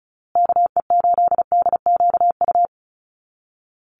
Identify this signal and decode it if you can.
Morse code, heard as a keyed tone, about 34 wpm, 700 Hz. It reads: KE8DQU
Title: USA Call Signs Repeated 3x 35wpm